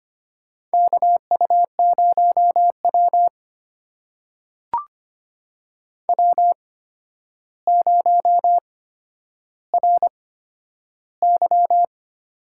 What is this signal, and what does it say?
Morse code, heard as a keyed tone, about 25 wpm, spaced as for 11 wpm, 700 Hz. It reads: KU0W W0RY
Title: USA Call Signs - 4 Characters - 8x Character Spacing - 25wpm